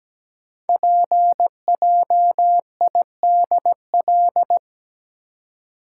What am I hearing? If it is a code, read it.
Morse code, heard as a keyed tone, about 17 wpm, 700 Hz. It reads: PJIDL
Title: Code Groups - 5 Letters - 17wpm